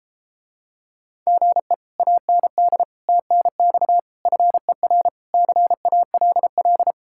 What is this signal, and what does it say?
Morse code, heard as a keyed tone, about 33 wpm, 700 Hz. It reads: GE AND TNX FER CALL